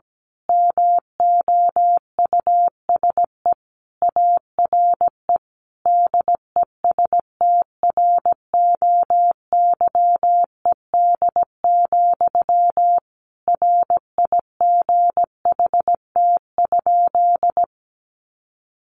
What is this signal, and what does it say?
Morse code, heard as a keyed tone, about 17 wpm, 700 Hz. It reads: MOUSE ARE DESTROYED, RIGHT?
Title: A Sound Of Thunder 17wpm